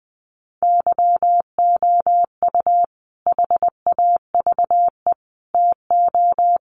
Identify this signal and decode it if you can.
Morse code, heard as a keyed tone, about 20 wpm, 700 Hz. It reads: YOU HAVE TO
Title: Sets of 5 Words Encore 20wpm